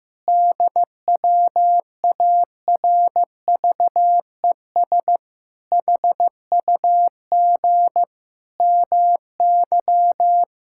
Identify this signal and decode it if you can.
Morse code, heard as a keyed tone, about 15 wpm, 700 Hz. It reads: DWARVES HUG MY
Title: Pangram Challenge v2 - 15wpm